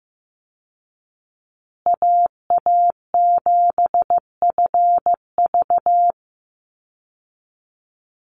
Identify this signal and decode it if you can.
Morse code, heard as a keyed tone, about 15 wpm, 700 Hz. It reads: AA7FV
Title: Call Signs Repeated 3x - 15wpm